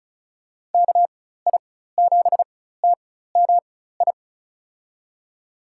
Morse code, heard as a keyed tone, about 36 wpm, spaced as for 16 wpm, 700 Hz. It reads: KI7TMI
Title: USA Call Signs - 6 Characters - 4x Character Spacing - 35wpm